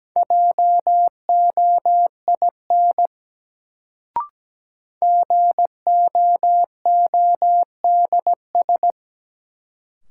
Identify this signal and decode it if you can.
Morse code, heard as a keyed tone, about 17 wpm, 700 Hz. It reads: JOIN GOODS
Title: Sentences from Top 1000 Words - Review Part 6 - New Word Drill - 17wpm